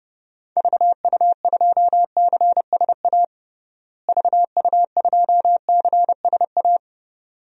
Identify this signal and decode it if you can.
Morse code, heard as a keyed tone, about 32 wpm, 700 Hz. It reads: VU2CSA VU2CSA
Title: DXCC-100 Call Signs - 6 Characters - Repeated 2x - 30wpm